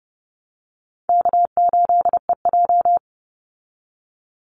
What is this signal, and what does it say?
Morse code, heard as a keyed tone, about 30 wpm, 700 Hz. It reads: K8EJ
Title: Call Signs Repeated 2x - 30wpm